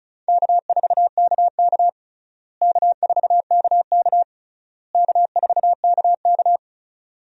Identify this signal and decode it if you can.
Morse code, heard as a keyed tone, about 36 wpm, 700 Hz. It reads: K4KK K4KK K4KK
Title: USA Call Signs - 4 Characters - Repeated 3x - 35wpm